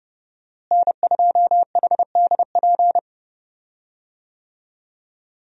Morse code, heard as a keyed tone, about 32 wpm, 700 Hz. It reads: N2HDP
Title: USA Call Signs - 5 Characters - Repeated 2x - 30wpm